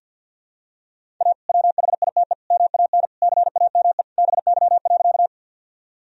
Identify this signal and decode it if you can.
Morse code, heard as a keyed tone, about 47 wpm, 700 Hz. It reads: A WHITE MAN CAME BY.
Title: Sentences from Top 1000 Words - Review Part 1 - 50wpm